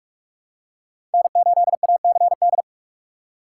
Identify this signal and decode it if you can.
Morse code, heard as a keyed tone, about 46 wpm, 700 Hz. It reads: N8ACD
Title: USA Call Signs - 3x Character Spacing - 45wpm